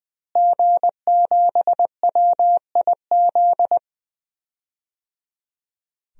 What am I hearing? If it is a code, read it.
Morse code, heard as a keyed tone, about 21 wpm, 700 Hz. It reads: G7WIZ
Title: DXCC-100 Call Signs - 5 Characters - Repeated 3x - 20wpm